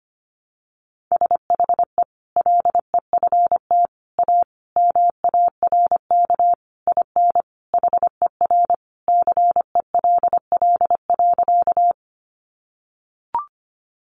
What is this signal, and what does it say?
Morse code, heard as a keyed tone, about 25 wpm, 700 Hz. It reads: SHE LEFT A MARK IN HER CELL.
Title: Sentences from Top 1100 Words 25wpm